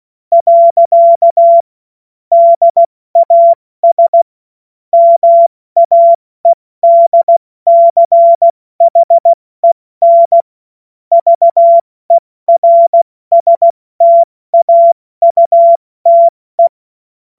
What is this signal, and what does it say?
Morse code, heard as a keyed tone, about 16 wpm, 680 Hz. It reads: . DAS MAEDCHEN VERSTAUTE
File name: Morsekurs / Witze / Witz 08. November 2022 70 F80